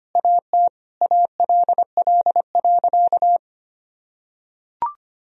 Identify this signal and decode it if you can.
Morse code, heard as a keyed tone, about 25 wpm, 700 Hz. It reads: AT ALL.
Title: Sentences from Top 1000 Words - Review Part 2 - 25wpm